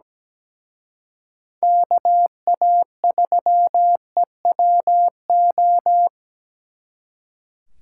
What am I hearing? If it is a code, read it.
Morse code, heard as a keyed tone, about 20 wpm, 700 Hz. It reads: KA3EWO
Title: Speed Racing - USA 2x3 Call Signs 17wpm